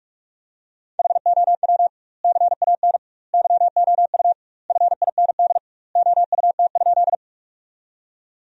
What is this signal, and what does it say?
Morse code, heard as a keyed tone, about 42 wpm, 700 Hz. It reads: HOW CAN YOU FIND OUT?
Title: 2x Word-Spacing - Sentences from Top 200 Words - 45wpm